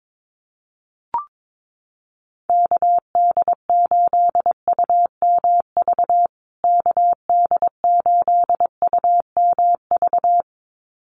Morse code, heard as a keyed tone, about 22 wpm, 700 Hz. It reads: KD8UMV KD8UMV
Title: USA Call Signs Repeated 2x 22wpm